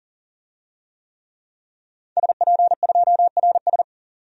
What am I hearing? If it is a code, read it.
Morse code, heard as a keyed tone, about 40 wpm, 700 Hz. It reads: SP2RS
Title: DXCC-100 Call Signs - 5 Characters - 40wpm